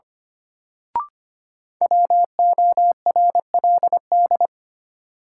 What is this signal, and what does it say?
Morse code, heard as a keyed tone, about 25 wpm, 700 Hz. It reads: WORLD
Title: Sets of 2 Words - New Word Drill - 25wpm